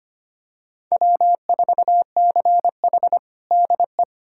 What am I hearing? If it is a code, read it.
Morse code, heard as a keyed tone, about 25 wpm, 700 Hz. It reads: W4CH DE
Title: USA Callsign DE Callsign - 4 Characters - 25wpm